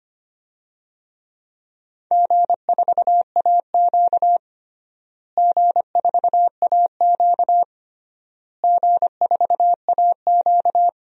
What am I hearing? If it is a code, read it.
Morse code, heard as a keyed tone, about 25 wpm, 700 Hz. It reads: G4AQ G4AQ G4AQ
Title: DXCC-100 Call Signs - 4 Characters - Repeated 3x - 25wpm